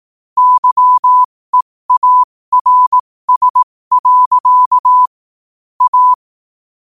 Morse code, heard as a keyed tone, about 18 wpm, spaced as for 15 wpm, 1000 Hz. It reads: YEARS. A
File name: qotdfiles / 15wpm